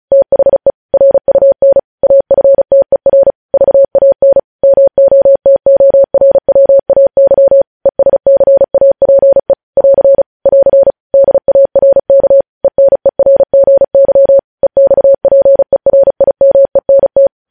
Morse code, heard as a keyed tone, about 36 wpm, 550 Hz. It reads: THE RUN AFTER VAN MOTORWAY ESCAPE <AR> <AR> DARK ENERGY EXPERIMENT
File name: podcast / News headlines-1x-35x35